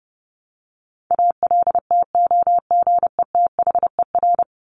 Morse code, heard as a keyed tone, about 28 wpm, 700 Hz. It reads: ALTOGETHER
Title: Top 1501-1750 Words 30wpm